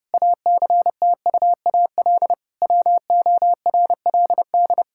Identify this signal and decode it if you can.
Morse code, heard as a keyed tone, about 30 wpm, 700 Hz. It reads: ACTUAL WORLD
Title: Sentences from Top 1000 Words - Review Part 5 - 30wpm